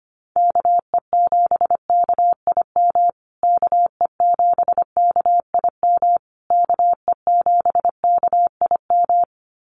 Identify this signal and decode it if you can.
Morse code, heard as a keyed tone, about 25 wpm, 700 Hz. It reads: KE7KIM KE7KIM KE7KIM
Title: USA Call Signs Repeated 3x 25wpm